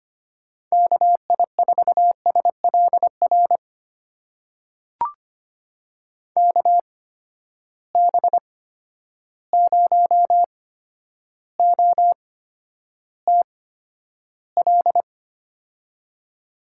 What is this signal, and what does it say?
Morse code, heard as a keyed tone, about 25 wpm, spaced as for 11 wpm, 700 Hz. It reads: KI4SLR KB0OTL
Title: USA Call Signs - 6 Characters - 8x Character Spacing - 25wpm